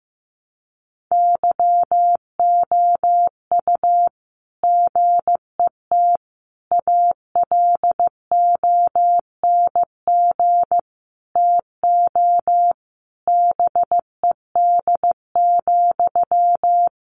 Morse code, heard as a keyed tone, about 15 wpm, 700 Hz. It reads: YOU GET ALONG TO BED,
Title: Sentences from Top 700 Words 15wpm